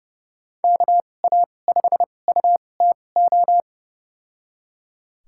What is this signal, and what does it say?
Morse code, heard as a keyed tone, about 30 wpm, spaced as for 22 wpm, 700 Hz. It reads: KA5UTO
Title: USA Call Signs - 6 Characters - 2x Character Spacing - 30wpm